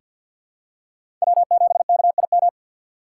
Morse code, heard as a keyed tone, about 46 wpm, 700 Hz. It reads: W7KIM
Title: USA Call Signs - 5 Characters - 2x Character Spacing - 50wpm